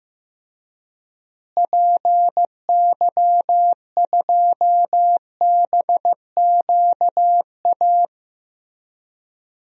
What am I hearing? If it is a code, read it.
Morse code, heard as a keyed tone, about 15 wpm, 700 Hz. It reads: PY2BQA
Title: DXCC-100 Call Signs - 6 Characters - Repeated 3x - 15wpm